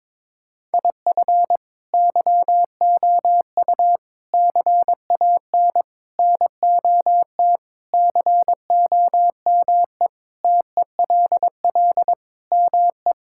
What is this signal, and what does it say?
Morse code, heard as a keyed tone, about 22 wpm, 700 Hz. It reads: IF YOU CAN NOT COME TELL ME
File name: practice / 22wpm / Sentences with 4-Word Phrases - 22wpm